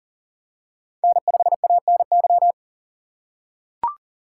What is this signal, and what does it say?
Morse code, heard as a keyed tone, about 37 wpm, 700 Hz. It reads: N5ANY
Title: USA Call Signs - 5 Characters - 3x Character Spacing - 40wpm